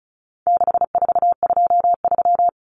Morse code, heard as a keyed tone, about 34 wpm, 700 Hz. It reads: 6423
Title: Numbers - 4-Digit 35wpm